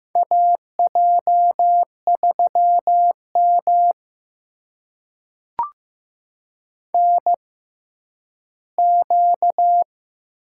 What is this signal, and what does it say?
Morse code, heard as a keyed tone, about 15 wpm, spaced as for 10 wpm, 700 Hz. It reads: AJ3M NQ
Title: USA Call Signs - 4 Characters - 6x Character Spacing - 15wpm